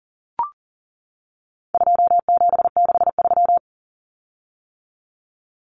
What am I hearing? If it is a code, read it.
Morse code, heard as a keyed tone, about 40 wpm, 700 Hz. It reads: 2763
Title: Common Cut Numbers - 4-Digit 40wpm